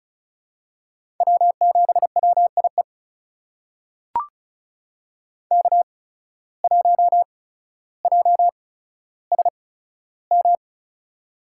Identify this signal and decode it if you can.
Morse code, heard as a keyed tone, about 36 wpm, spaced as for 15 wpm, 700 Hz. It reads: W7WIE K1JSM
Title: USA Call Signs - 5 Characters - 8x Character Spacing - 35wpm